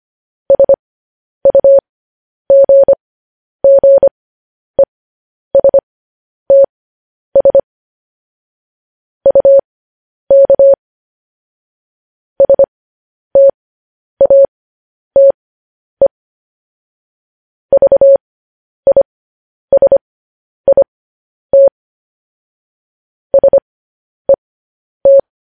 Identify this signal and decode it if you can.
Morse code, heard as a keyed tone, about 25 wpm, spaced as for 10 wpm, 550 Hz. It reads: SUGGESTS UK STATE VISIT SET
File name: podcast / News headlines-1x-25x10